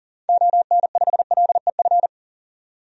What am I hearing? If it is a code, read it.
Morse code, heard as a keyed tone, about 37 wpm, 700 Hz. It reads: ON5LEF
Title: DXCC-100 Call Signs - 6 Characters - 40wpm